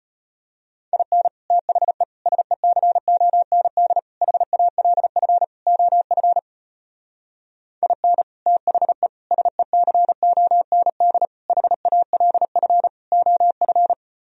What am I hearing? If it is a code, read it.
Morse code, heard as a keyed tone, about 32 wpm, 700 Hz. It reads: IN THE SECOND HALF OF IN THE SECOND HALF OF